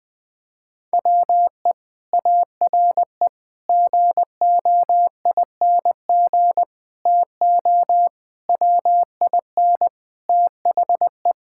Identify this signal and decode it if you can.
Morse code, heard as a keyed tone, about 20 wpm, 700 Hz. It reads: WE ARE GOING TO WIN THE
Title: Sentences from Top 1600 Words 20wpm